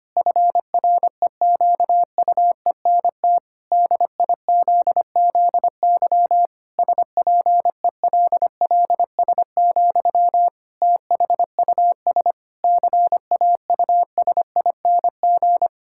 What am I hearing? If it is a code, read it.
Morse code, heard as a keyed tone, about 25 wpm, 700 Hz. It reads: FREQUENT DIZZY SPELLS, THUS CAUSING
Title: Pangram Challenge v2 - 25wpm